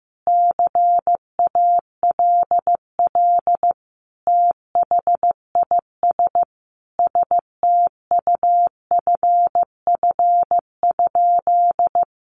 Morse code, heard as a keyed tone, about 15 wpm, 700 Hz. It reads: CALL THIS STUFF?
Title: Sentences from Top 1400 Words 15wpm